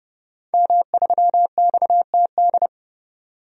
Morse code, heard as a keyed tone, about 31 wpm, 700 Hz. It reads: M3XTD
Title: DXCC-100 Call Signs - 5 Characters - 30wpm